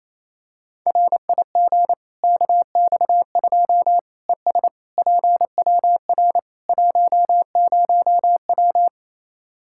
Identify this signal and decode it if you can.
Morse code, heard as a keyed tone, about 27 wpm, 700 Hz. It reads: RIG KX2 ES PWR 10W
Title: K1USN SST QSO Practice 28wpm